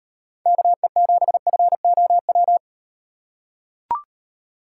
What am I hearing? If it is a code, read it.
Morse code, heard as a keyed tone, about 38 wpm, 700 Hz. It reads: KE7FOW